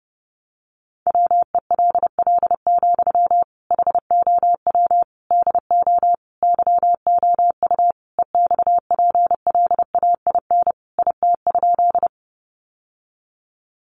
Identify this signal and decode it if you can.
Morse code, heard as a keyed tone, about 30 wpm, 700 Hz. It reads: WELL, HOW DO YOU EXPLAIN IT?
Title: Sentences from Top 1100 Words 30wpm